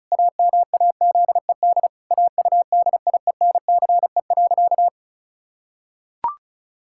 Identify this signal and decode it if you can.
Morse code, heard as a keyed tone, about 34 wpm, 700 Hz. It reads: AMAZED AUDIENCE.
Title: Pangram Challenge v2 - 35wpm